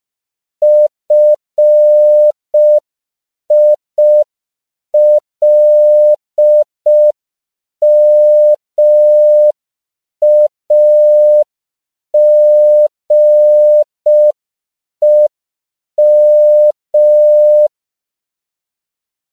Morse code, heard as a keyed tone, about 5 wpm, 600 Hz. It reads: FILMAGEM
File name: CW / 5PPM